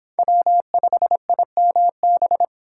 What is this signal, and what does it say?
Morse code, heard as a keyed tone, about 25 wpm, 700 Hz. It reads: W5IMB